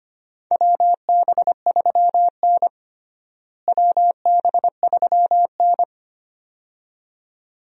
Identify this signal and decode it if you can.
Morse code, heard as a keyed tone, about 25 wpm, 700 Hz. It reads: WB3N WB3N
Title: USA Call Signs - 4 Characters - Repeated 2x - 25wpm